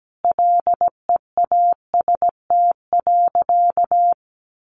Morse code, heard as a keyed tone, about 17 wpm, 700 Hz. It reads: LEAST.
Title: Sentences from Top 900 Words 17wpm